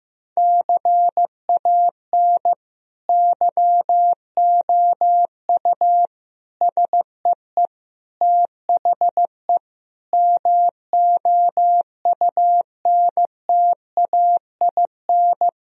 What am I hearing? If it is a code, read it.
Morse code, heard as a keyed tone, about 15 wpm, 700 Hz. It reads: CAN YOU SEE THE MOUNTAIN
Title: Sentences from Top 1000 Words - Review Part 1 - 15wpm